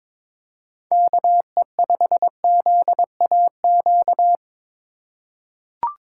Morse code, heard as a keyed tone, about 22 wpm, 700 Hz. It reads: KE5ZAQ
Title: USA Call Signs - 6 Characters - 3x Character Spacing - 22wpm